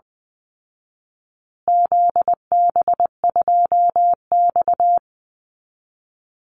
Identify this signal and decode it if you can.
Morse code, heard as a keyed tone, about 20 wpm, 700 Hz. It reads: ZB2X
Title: Call Signs Repeated 2x - 20wpm